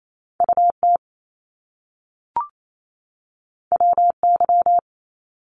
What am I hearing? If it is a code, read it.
Morse code, heard as a keyed tone, about 28 wpm, 700 Hz. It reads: UT WY
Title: ARRL Sections 28wpm